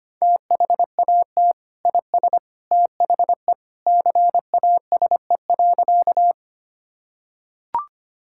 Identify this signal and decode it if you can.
Morse code, heard as a keyed tone, about 25 wpm, 700 Hz. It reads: THAT IS THE CASE.
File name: practice / 25wpm / Sentences with 2-Word Phrases - 25wpm